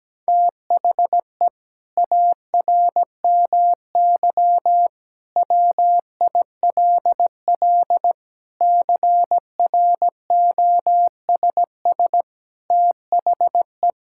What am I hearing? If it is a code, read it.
Morse code, heard as a keyed tone, about 17 wpm, 700 Hz. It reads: THE ARMY WILL CROSS THE
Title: Sentences from Top 1000 Words - Review Part 5 - 17wpm